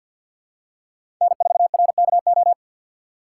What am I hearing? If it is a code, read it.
Morse code, heard as a keyed tone, about 46 wpm, 700 Hz. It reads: N4RKO
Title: USA Call Signs - 5 Characters - 3x Character Spacing - 50wpm